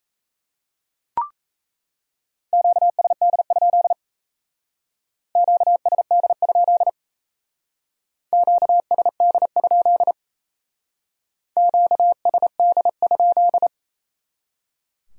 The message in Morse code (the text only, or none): QSD? QSD? QSD? QSD?